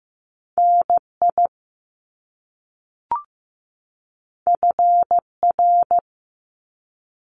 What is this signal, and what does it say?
Morse code, heard as a keyed tone, about 15 wpm, 700 Hz. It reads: NI FR
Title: Top Two Letter Combinations in Words 15wpm